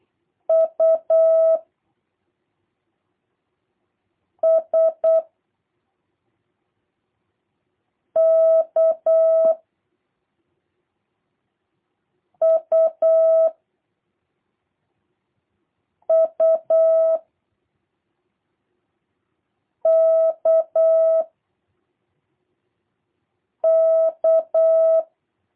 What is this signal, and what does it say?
Morse code, heard as a keyed tone, about 8 wpm, 660 Hz. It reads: U S K U U K K
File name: jh8cbh / 2023cw / review1-3